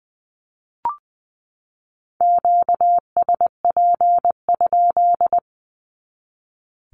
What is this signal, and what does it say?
Morse code, heard as a keyed tone, about 20 wpm, 700 Hz. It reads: QSP?
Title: Q-Codes 20wpm